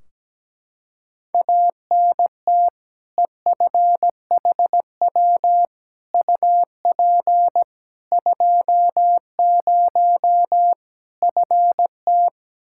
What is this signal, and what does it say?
Morse code, heard as a keyed tone, about 17 wpm, 700 Hz. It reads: ANT EFHW UP 20 FT
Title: Practice QSO - Exchange 2 - Info - 17wpm